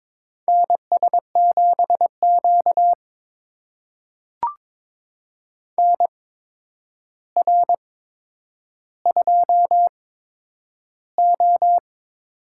Morse code, heard as a keyed tone, about 22 wpm, spaced as for 9 wpm, 700 Hz. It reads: NS7Q NR2O
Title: USA Call Signs - 4 Characters - 8x Character Spacing - 22wpm